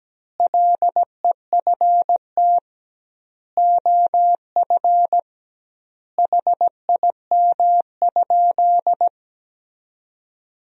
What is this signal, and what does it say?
Morse code, heard as a keyed tone, about 17 wpm, 700 Hz. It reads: LEFT OF HIM?
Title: 2x Word-Spacing - Sentences from Top 300 Words - 17wpm